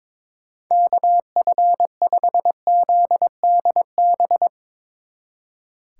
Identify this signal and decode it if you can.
Morse code, heard as a keyed tone, about 23 wpm, 700 Hz. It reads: KF5ZDB
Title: Speed Racing - USA 2x3 Call Signs 22wpm